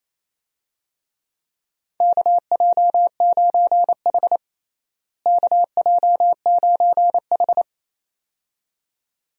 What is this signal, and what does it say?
Morse code, heard as a keyed tone, about 28 wpm, 700 Hz. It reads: KJ9H KJ9H
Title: USA Call Signs - 4 Characters - Repeated 2x - 28wpm